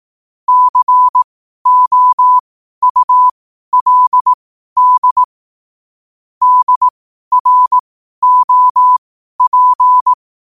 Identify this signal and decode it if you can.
Morse code, heard as a keyed tone, about 18 wpm, spaced as for 12 wpm, 1000 Hz. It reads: COULD DROP
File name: qotdfiles / 12wpm